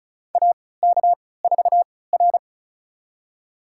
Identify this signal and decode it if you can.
Morse code, heard as a keyed tone, about 34 wpm, spaced as for 20 wpm, 700 Hz. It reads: AK4R
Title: USA Call Signs - 4 Characters - 3x Character Spacing - 35wpm